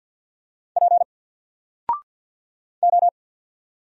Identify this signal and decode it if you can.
Morse code, heard as a keyed tone, about 50 wpm, 700 Hz. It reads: P O
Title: Single Character (Letter:Number:Punctuation:Prosign) 50wpm